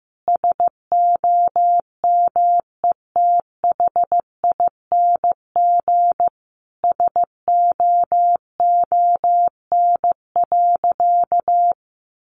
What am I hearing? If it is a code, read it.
Morse code, heard as a keyed tone, about 15 wpm, 700 Hz. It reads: SOMETHING SOON.
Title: Sentences from Top 1500 Words 15wpm